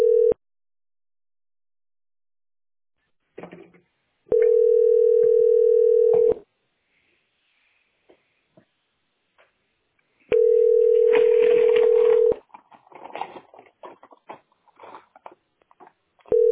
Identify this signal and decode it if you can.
Ringback tone (North America), 440 + 480 Hz.
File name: cce6d750-3e48-4427-82db